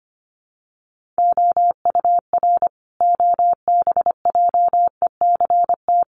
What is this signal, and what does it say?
Morse code, heard as a keyed tone, about 25 wpm, 700 Hz. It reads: OUR OBJECT
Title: Sentences from Top 1500 Words 25wpm